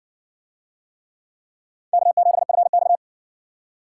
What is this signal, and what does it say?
Morse code, heard as a keyed tone, about 60 wpm, 700 Hz. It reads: K7FX